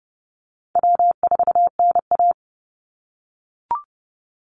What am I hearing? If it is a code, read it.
Morse code, heard as a keyed tone, about 30 wpm, 700 Hz. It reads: W4NA
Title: Easy Call Signs - 30wpm